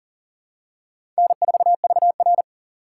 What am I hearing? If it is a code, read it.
Morse code, heard as a keyed tone, about 40 wpm, 700 Hz. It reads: N4VR
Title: USA Call Signs - 4 Characters - 4x Character Spacing - 40wpm